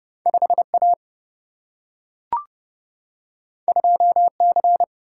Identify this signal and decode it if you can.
Morse code, heard as a keyed tone, about 30 wpm, 700 Hz. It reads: 5 A 2 C
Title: Single-Letter Single-Number 30wpm